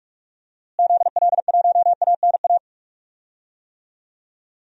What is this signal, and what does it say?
Morse code, heard as a keyed tone, about 44 wpm, 700 Hz. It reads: ZL1ANA
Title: DXCC-100 Call Signs - 6 Characters - Repeated 3x - 45wpm